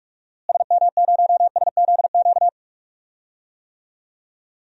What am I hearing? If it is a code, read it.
Morse code, heard as a keyed tone, about 42 wpm, 700 Hz. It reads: SM0SZQ
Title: DXCC-100 Call Signs - 6 Characters - Repeated 3x - 45wpm